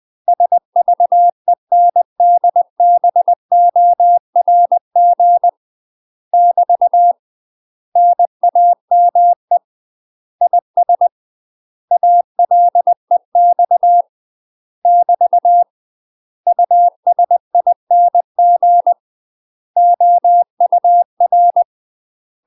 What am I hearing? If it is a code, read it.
Morse code, heard as a keyed tone, about 20 wpm, 700 Hz. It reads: SVENDBORG <BT> NAME IS ALEX <BT> USING OUR
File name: UK MASTER QSOs TRAINING/20WPM